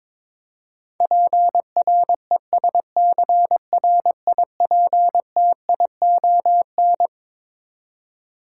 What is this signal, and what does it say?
Morse code, heard as a keyed tone, about 22 wpm, 700 Hz. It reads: PRESCRIPTION
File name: Twelve Letter Words - 22wpm